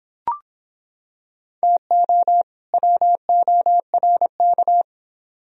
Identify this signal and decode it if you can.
Morse code, heard as a keyed tone, about 26 wpm, 700 Hz. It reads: TO WORK